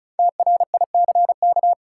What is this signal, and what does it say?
Morse code, heard as a keyed tone, about 36 wpm, 700 Hz. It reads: TRICK
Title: Five Letter Words - 35wpm